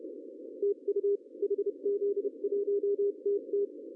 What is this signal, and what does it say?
Morse code, heard as a keyed tone, about 29 wpm, 440 Hz. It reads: TU HZ1TT
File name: CW / cqww-cw-k5tr